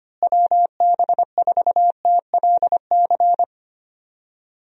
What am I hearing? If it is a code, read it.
Morse code, heard as a keyed tone, about 25 wpm, 700 Hz. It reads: WB4TLC
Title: USA Call Signs - 6 Characters - Repeated 2x - 25wpm